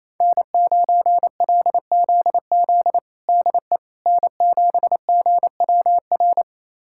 Morse code, heard as a keyed tone, about 28 wpm, 700 Hz. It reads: N9LZZ DE N7GWR
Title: USA Callsign DE Callsign - 5 Characters - 28wpm